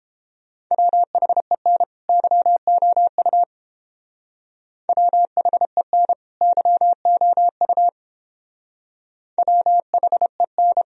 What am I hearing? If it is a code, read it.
Morse code, heard as a keyed tone, about 30 wpm, 700 Hz. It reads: WHEN YOU WHEN YOU WHEN